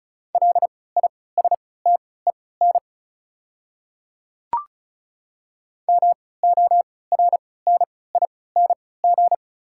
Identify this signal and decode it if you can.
Morse code, heard as a keyed tone, about 36 wpm, spaced as for 20 wpm, 700 Hz. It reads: LISTEN MORNING
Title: 3x Character Spacing - Top 500 Words - 35wpm